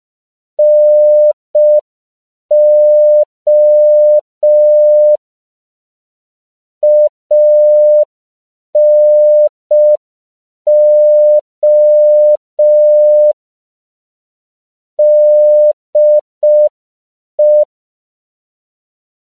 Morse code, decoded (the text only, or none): NO ANO DE